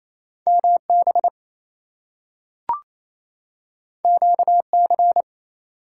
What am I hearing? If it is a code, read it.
Morse code, heard as a keyed tone, about 28 wpm, 700 Hz. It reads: MB QC
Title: US States and Canadian Provinces 28wpm